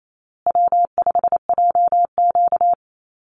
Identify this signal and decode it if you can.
Morse code, heard as a keyed tone, about 28 wpm, 700 Hz. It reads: W5JQ
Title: Call Signs Repeated 2x - 28wpm